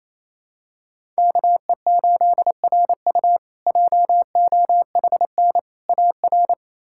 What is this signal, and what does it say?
Morse code, heard as a keyed tone, about 27 wpm, 700 Hz. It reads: KE8RU JOHN AR
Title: K1USN SST QSO Practice - 28wpm